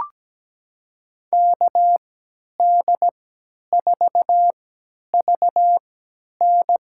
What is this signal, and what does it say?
Morse code, heard as a keyed tone, about 17 wpm, spaced as for 10 wpm, 700 Hz. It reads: KD4VN
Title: USA Call Signs - 3x Character Spacing - 17wpm